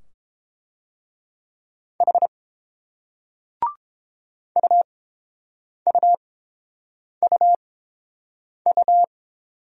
Morse code, heard as a keyed tone, about 30 wpm, spaced as for 7 wpm, 700 Hz. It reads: H UUUU